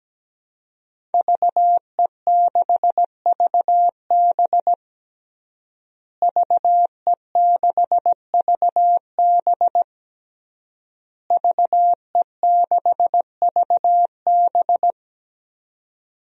Morse code, heard as a keyed tone, about 17 wpm, 700 Hz. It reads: VE6VB VE6VB VE6VB
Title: DXCC-100 Call Signs - 5 Characters - Repeated 3x - 17wpm